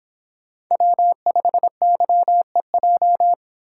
Morse code, heard as a keyed tone, about 26 wpm, 700 Hz. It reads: W5YEJ